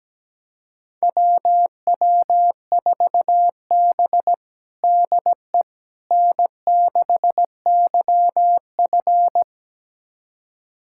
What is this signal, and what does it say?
Morse code, heard as a keyed tone, about 17 wpm, 700 Hz. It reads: WW4B DE N6YF
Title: USA Callsign DE Callsign - 4 Characters - 17wpm